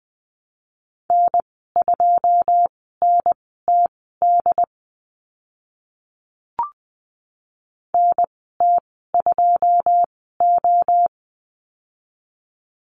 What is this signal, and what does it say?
Morse code, heard as a keyed tone, about 20 wpm, spaced as for 14 wpm, 700 Hz. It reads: N2NTD NT2O
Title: 082 Lesson 14g - Number 2 - Call Signs 20wpm